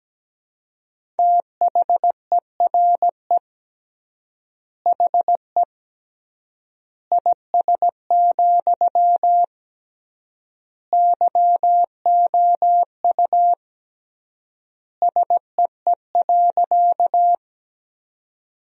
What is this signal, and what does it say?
Morse code, heard as a keyed tone, about 17 wpm, 700 Hz. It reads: THERE HE IS, YOU SEE.
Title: 3x Word-Spacing - Sentences from Top 100 Words - 17wpm